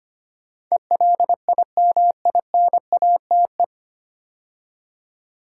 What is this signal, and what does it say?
Morse code, heard as a keyed tone, about 25 wpm, 700 Hz. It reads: ELIMINATE
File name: practice / 25wpm / 9 Letter Words - First 3 Letters Given - 25wpm